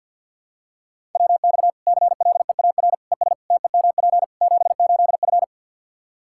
Morse code, heard as a keyed tone, about 52 wpm, 700 Hz. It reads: WX CLEAR ES TEMP 78F
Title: K1USN SST QSO Practice 50wpm